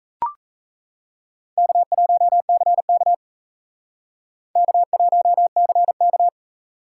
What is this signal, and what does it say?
Morse code, heard as a keyed tone, about 37 wpm, 700 Hz. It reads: K1CK K1CK